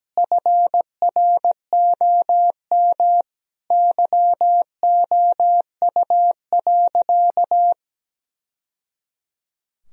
Morse code, heard as a keyed tone, about 17 wpm, 700 Hz. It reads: FROM YOU.
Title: Sentences from Top 1000 Words - Review Part 2 - 17wpm